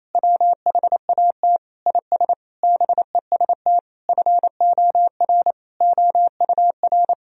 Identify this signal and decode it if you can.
Morse code, heard as a keyed tone, about 28 wpm, 700 Hz. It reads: WHAT IS BEST FOR OUR
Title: Sentences from Top 1000 Words - Review Part 4 - 28wpm